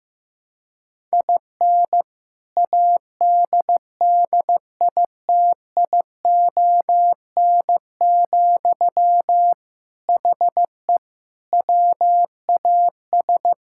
Morse code, heard as a keyed tone, about 15 wpm, 700 Hz. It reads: IN ADDITION, HE WAS
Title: Sentences from Top 1000 Words - Review Part 6 - 15wpm